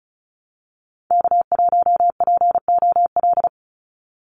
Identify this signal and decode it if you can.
Morse code, heard as a keyed tone, about 34 wpm, 700 Hz. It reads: K1POL
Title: USA Call Signs 35wpm